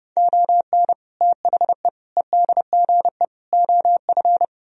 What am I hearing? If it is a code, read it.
Morse code, heard as a keyed tone, about 35 wpm, 700 Hz. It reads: ON THE EDGE OF
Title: Speed-Racing - Sets of 4 Words 30wpm